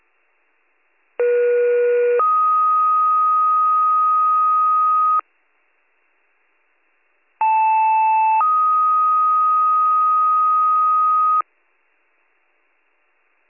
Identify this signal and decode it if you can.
Two-tone page, heard as a one-second tone, then a three-second tone, about 480 then 1250 Hz, about 880 then 1250 Hz. It is heard from 1.2 to 11.4 s.